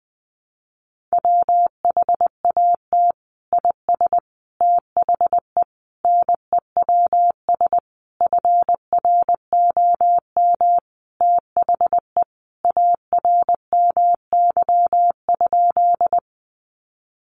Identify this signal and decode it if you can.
Morse code, heard as a keyed tone, about 20 wpm, 700 Hz. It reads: WHAT IS THE NEWS FROM THE ARMY?
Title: Sentences from Top 800 Words 20wpm